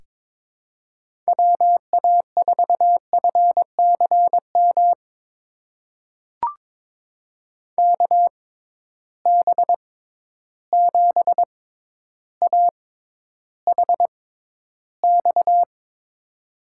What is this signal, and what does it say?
Morse code, heard as a keyed tone, about 22 wpm, spaced as for 11 wpm, 700 Hz. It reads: WA4FCM KB7AHX
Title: USA Call Signs - 6x Character Spacing - 22wpm